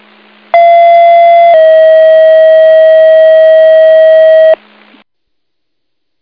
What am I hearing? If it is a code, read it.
Two-tone page, heard as a one-second tone, then a three-second tone, about 680 then 640 Hz. It is heard from 0.5 to 4.5 s.